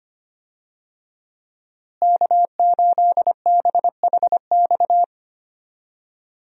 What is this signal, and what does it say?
Morse code, heard as a keyed tone, about 25 wpm, 700 Hz. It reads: K8BHX
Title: USA Call Signs - 5 Characters - Repeated 3x - 25wpm